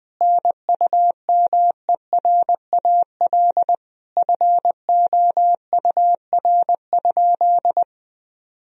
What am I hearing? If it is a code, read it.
Morse code, heard as a keyed tone, about 20 wpm, 700 Hz. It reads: NUMERAL FOUR?
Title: Sentences from Top 1000 Words - Review Part 2 - 20wpm